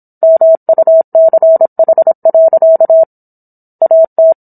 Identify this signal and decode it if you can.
Morse code, heard as a keyed tone, about 27 wpm, 650 Hz. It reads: MUCH. AT